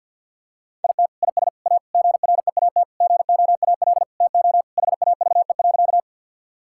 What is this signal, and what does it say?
Morse code, heard as a keyed tone, about 50 wpm, 700 Hz. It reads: IT IS A GREAT GOAL TO HAVE.
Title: Sentences from Top 1000 Words - Review Part 5 - 50wpm